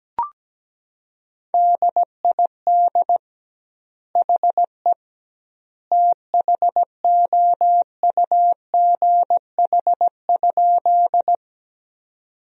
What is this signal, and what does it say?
Morse code, heard as a keyed tone, about 17 wpm, 700 Hz. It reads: DID HE THOUGH?
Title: 2x Word-Spacing - Sentences from Top 400 Words - 17wpm